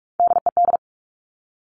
Morse code, heard as a keyed tone, about 46 wpm, 700 Hz. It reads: DED
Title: Top Three Letter Combinations in Words 45wpm